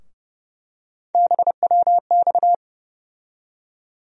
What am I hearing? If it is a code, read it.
Morse code, heard as a keyed tone, about 32 wpm, 700 Hz. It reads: BWX
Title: USA Call Sign Postfixes - 3 Letters 30wpm